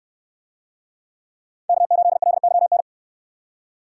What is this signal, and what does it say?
Morse code, heard as a keyed tone, about 65 wpm, 700 Hz. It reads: K8LYN